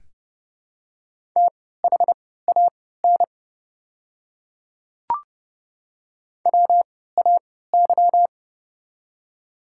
Morse code, heard as a keyed tone, about 30 wpm, spaced as for 17 wpm, 700 Hz. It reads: THAN WAY
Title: 3x Character Spacing - Top 100 Words - 30wpm